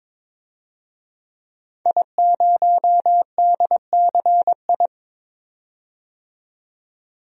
Morse code, heard as a keyed tone, about 22 wpm, 700 Hz. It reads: I0DCI
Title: DXCC-100 Call Signs - 5 Characters - 22wpm